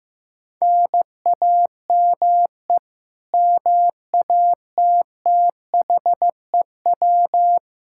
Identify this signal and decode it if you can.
Morse code, heard as a keyed tone, about 16 wpm, 700 Hz. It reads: NAME MATTHEW
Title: K1USN SST QSO Practice 15wpm